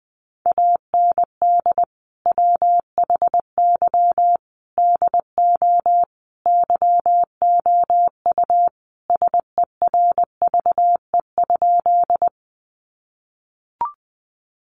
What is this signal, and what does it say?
Morse code, heard as a keyed tone, about 20 wpm, 700 Hz. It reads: AND WHY DO YOU SERVE?
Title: Sentences from Top 400 Words 20wpm